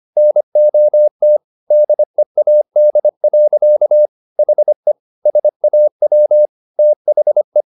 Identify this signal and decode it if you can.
Morse code, heard as a keyed tone, about 25 wpm, 600 Hz. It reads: NOT DEAD. HE SAW THE